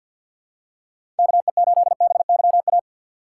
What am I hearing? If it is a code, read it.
Morse code, heard as a keyed tone, about 47 wpm, 700 Hz. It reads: KE8BYA
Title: USA Call Signs - 6 Characters - 4x Character Spacing - 50wpm